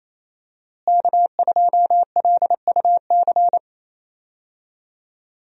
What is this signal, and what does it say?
Morse code, heard as a keyed tone, about 28 wpm, 700 Hz. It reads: K2LUC
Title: USA Call Signs - 5 Characters - Repeated 3x - 28wpm